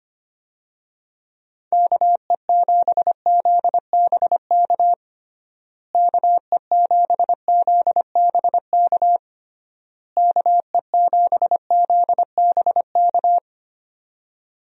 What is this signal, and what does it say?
Morse code, heard as a keyed tone, about 25 wpm, 700 Hz. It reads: KE7ZBK KE7ZBK KE7ZBK
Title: USA Call Signs - 6 Characters - Repeated 3x - 25wpm